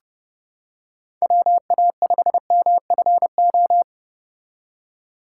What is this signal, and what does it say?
Morse code, heard as a keyed tone, about 31 wpm, 700 Hz. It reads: WA5MFO